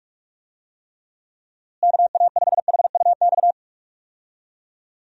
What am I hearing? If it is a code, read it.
Morse code, heard as a keyed tone, about 41 wpm, 700 Hz. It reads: KA5HUX
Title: USA Call Signs - 6 Characters - Repeated 2x - 45wpm